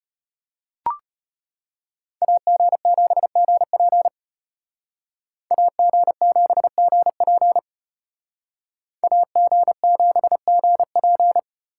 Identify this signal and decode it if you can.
Morse code, heard as a keyed tone, about 33 wpm, 700 Hz. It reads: AG7GP AG7GP AG7GP